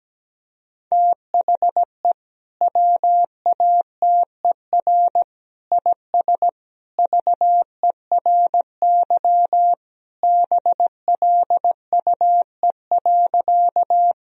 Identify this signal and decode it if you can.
Morse code, heard as a keyed tone, about 17 wpm, 700 Hz. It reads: THE WATER IS VERY BLUE.
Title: Sentences from Top 1000 Words - Review Part 3 - 17wpm